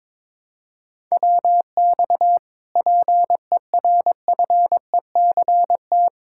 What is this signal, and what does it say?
Morse code, heard as a keyed tone, about 22 wpm, 700 Hz. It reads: WX PERFECT
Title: Speed-Racing Common QSO Phrases - 22wpm